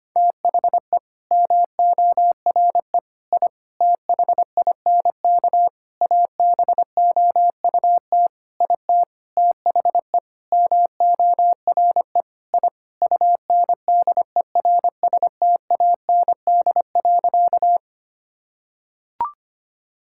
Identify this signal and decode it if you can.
Morse code, heard as a keyed tone, about 25 wpm, 700 Hz. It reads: THE MORE I THINK ABOUT IT THE MORE I UNDERSTAND.
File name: practice / 25wpm / Sentences with 9-Word Phrases - 25wpm